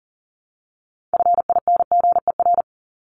Morse code, heard as a keyed tone, about 40 wpm, 700 Hz. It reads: FINGER
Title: Top 1501-1750 Words 40wpm